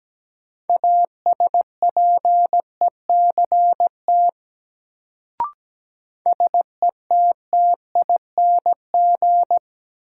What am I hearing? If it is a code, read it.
Morse code, heard as a keyed tone, about 17 wpm, 700 Hz. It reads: ASPECT SETTING
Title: Sentences from Top 1300 Words - New Word Drill - 17wpm